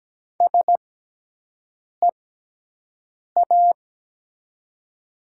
Morse code, heard as a keyed tone, about 17 wpm, spaced as for 6 wpm, 700 Hz. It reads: SEA
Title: 6x Character Spacing - Top 300 Words - 17wpm